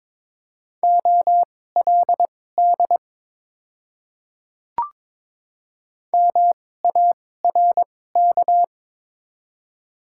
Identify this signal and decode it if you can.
Morse code, heard as a keyed tone, about 22 wpm, spaced as for 16 wpm, 700 Hz. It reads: OLD MARK
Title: 2x Character Spacing - Top 300 Words - 22wpm